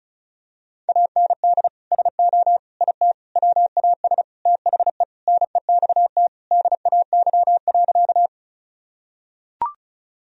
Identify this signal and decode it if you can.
Morse code, heard as a keyed tone, about 36 wpm, 700 Hz. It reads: AND SO IT WAS THE NEXT DAY.
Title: Sentences from Top 1000 Words - Review Part 1 - 35wpm